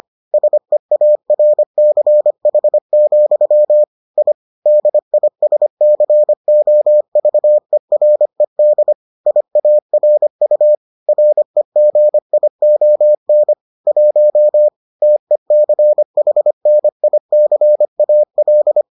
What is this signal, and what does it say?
Morse code, heard as a keyed tone, about 25 wpm, 600 Hz. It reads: SEARCH, I DISCOVERED IARU REGION 1 TECHNICAL